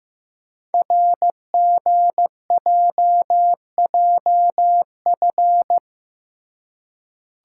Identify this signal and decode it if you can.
Morse code, heard as a keyed tone, about 15 wpm, 700 Hz. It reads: RGJJF
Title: Code Groups - 5 Letters - 15wpm